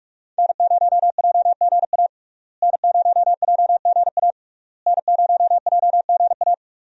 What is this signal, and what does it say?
Morse code, heard as a keyed tone, about 45 wpm, 700 Hz. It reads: N0JGA N0JGA N0JGA
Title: USA Call Signs - 5 Characters - Repeated 3x - 45wpm